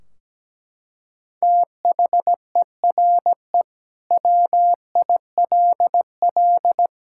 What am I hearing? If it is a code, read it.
Morse code, heard as a keyed tone, about 17 wpm, 700 Hz. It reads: THERE WILL
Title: Sentences from Top 1600 Words 17wpm